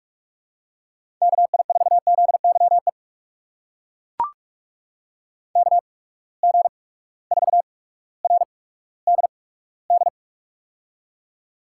Morse code, heard as a keyed tone, about 42 wpm, spaced as for 19 wpm, 700 Hz. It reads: KI4ZYE KG4RDD
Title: USA Call Signs - 6 Characters - 8x Character Spacing - 45wpm